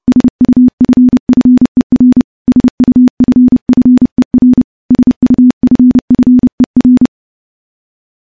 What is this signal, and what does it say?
Morse code, heard as a keyed tone, about 30 wpm, 250 Hz. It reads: SUFFER SUFFER SUFFER
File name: morse / morse-training-30WPM